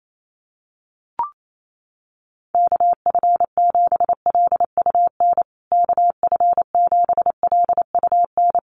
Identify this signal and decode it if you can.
Morse code, heard as a keyed tone, about 28 wpm, 700 Hz. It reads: KF7LUN KF7LUN
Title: USA Call Signs Repeated 3x 28wpm